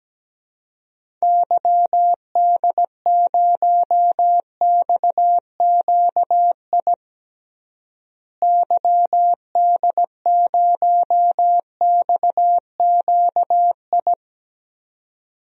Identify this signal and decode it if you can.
Morse code, heard as a keyed tone, about 17 wpm, 700 Hz. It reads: YD0XQI YD0XQI
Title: DXCC-100 Call Signs - 6 Characters - Repeated 2x - 17wpm